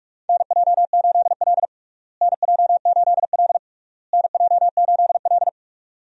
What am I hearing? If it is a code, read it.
Morse code, heard as a keyed tone, about 47 wpm, 700 Hz. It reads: NJ8L NJ8L NJ8L
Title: USA Call Signs - 4 Characters - Repeated 3x - 45wpm